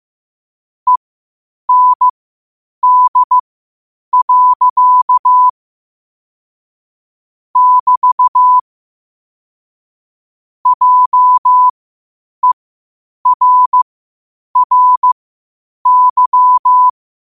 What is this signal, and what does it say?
Morse code, heard as a keyed tone, about 15 wpm, spaced as for 8 wpm, 1000 Hz. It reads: END. <BT> JERRY
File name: qotdfiles / 8wpm